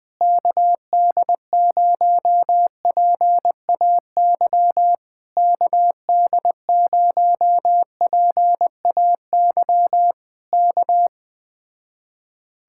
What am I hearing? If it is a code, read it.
Morse code, heard as a keyed tone, about 20 wpm, 700 Hz. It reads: KD0PAY KD0PAY K
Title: Speed-Racing Common QSO Phrases Encore - 17wpm